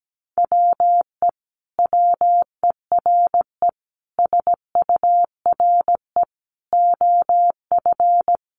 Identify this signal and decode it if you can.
Morse code, heard as a keyed tone, about 17 wpm, 700 Hz. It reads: WE WERE SURE OF
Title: Sentences from Top 1100 Words 17wpm